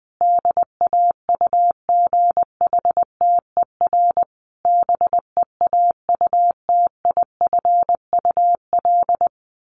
Morse code, heard as a keyed tone, about 20 wpm, 700 Hz. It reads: DAUGHTER BEAUTIFUL
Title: Sentences from Top 1500 Words 20wpm